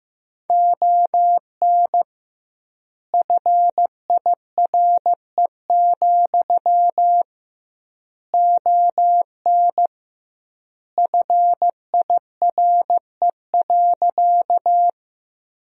Morse code, heard as a keyed tone, about 15 wpm, 700 Hz. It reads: ON FIRE, ON FIRE.
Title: 2x Word-Spacing - Sentences from Top 400 Words - 15wpm